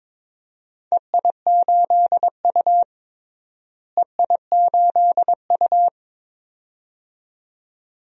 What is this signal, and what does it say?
Morse code, heard as a keyed tone, about 22 wpm, 700 Hz. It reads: EI8U EI8U
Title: DXCC-100 Call Signs - 4 Characters - Repeated 2x - 22wpm